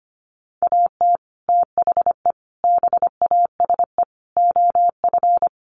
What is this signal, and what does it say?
Morse code, heard as a keyed tone, about 25 wpm, 700 Hz. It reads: AT THE BASE OF
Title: Sets of 5 Words 25wpm